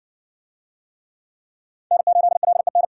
Morse code, heard as a keyed tone, about 57 wpm, 700 Hz. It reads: N8LEN